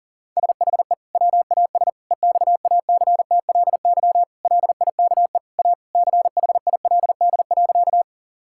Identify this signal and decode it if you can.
Morse code, heard as a keyed tone, about 40 wpm, 700 Hz. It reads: SHE WAS EXACTLY LIKE A CHILD.
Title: Sentences from Top 1000 Words - Review Part 6 - 40wpm